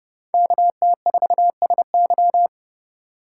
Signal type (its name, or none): Morse code